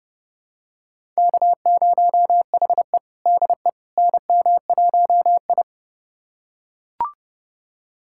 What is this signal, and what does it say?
Morse code, heard as a keyed tone, about 30 wpm, 700 Hz. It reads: K0HE DE NM1I
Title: USA Callsign DE Callsign - 4 Characters - 30wpm